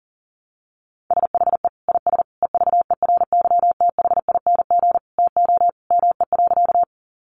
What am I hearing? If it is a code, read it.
Morse code, heard as a keyed tone, about 40 wpm, 700 Hz. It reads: SHE IS EVERYTHING TO ME.
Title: Sentences from Top 700 Words 40wpm